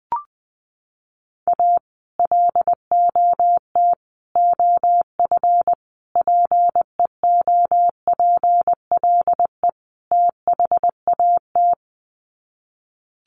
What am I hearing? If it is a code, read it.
Morse code, heard as a keyed tone, about 20 wpm, 700 Hz. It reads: A LOT OF PEOPLE THAT
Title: Sets of 5 Words Encore2 20wpm